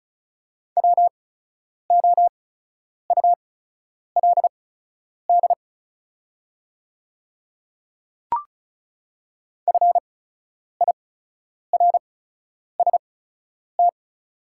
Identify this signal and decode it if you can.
Morse code, heard as a keyed tone, about 36 wpm, spaced as for 10 wpm, 700 Hz. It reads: WOULD FIRST
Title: 8x Character Spacing - Top 100 Words - 35wpm